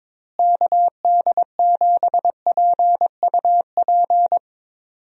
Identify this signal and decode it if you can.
Morse code, heard as a keyed tone, about 22 wpm, 700 Hz. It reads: KD7PUP
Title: USA Call Signs - 6 Characters - Repeated 2x - 22wpm